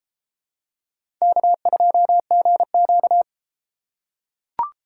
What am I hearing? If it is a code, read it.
Morse code, heard as a keyed tone, about 33 wpm, 700 Hz. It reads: K2GQ